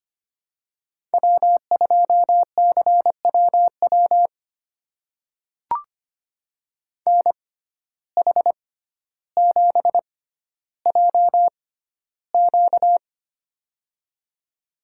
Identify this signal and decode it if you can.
Morse code, heard as a keyed tone, about 25 wpm, spaced as for 13 wpm, 700 Hz. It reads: W2CWW NH7JQ
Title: USA Call Signs - 5 Characters - 6x Character Spacing - 25wpm